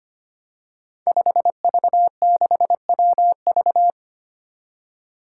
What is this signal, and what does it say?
Morse code, heard as a keyed tone, about 25 wpm, 700 Hz. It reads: 5V6WV
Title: Five Challenging Letter-Number Combinations 25wpm